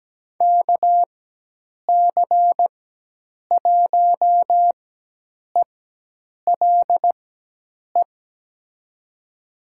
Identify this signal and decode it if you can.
Morse code, heard as a keyed tone, about 17 wpm, spaced as for 8 wpm, 700 Hz. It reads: KC1ELE
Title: USA Call Signs - 4x Character Spacing - 17wpm